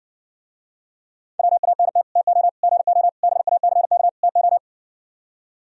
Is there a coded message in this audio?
Yes: Morse code from 1.4 to 4.6 s, heard as a keyed tone, about 60 wpm, 700 Hz. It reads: WANT TO GO BACK TO